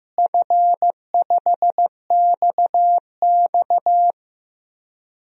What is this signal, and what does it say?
Morse code, heard as a keyed tone, about 15 wpm, 700 Hz. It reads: F5XX
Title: DXCC-100 Call Signs - 4 Characters - Repeated 2x - 15wpm